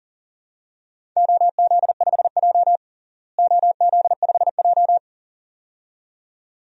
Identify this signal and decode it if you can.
Morse code, heard as a keyed tone, about 40 wpm, 700 Hz. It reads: OZ5J OZ5J
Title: DXCC-100 Call Signs - 4 Characters - Repeated 2x - 40wpm